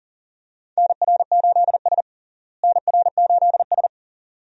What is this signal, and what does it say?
Morse code, heard as a keyed tone, about 40 wpm, 700 Hz. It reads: NR8S NR8S
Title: USA Call Signs - 4 Characters - Repeated 2x - 40wpm